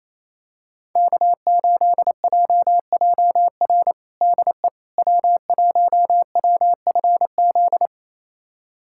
Morse code, heard as a keyed tone, about 28 wpm, 700 Hz. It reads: K8JJR DE W1WFZ
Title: USA Callsign DE Callsign - 5 Characters - 28wpm